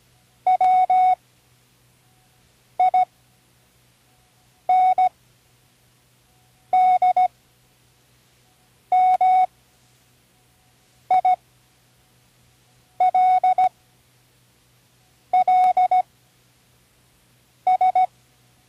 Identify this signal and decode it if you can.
Morse code, heard as a keyed tone, about 16 wpm, spaced as for 5 wpm, 740 Hz. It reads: WINDMILLS
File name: Learning Morse Code (from Tune in the World)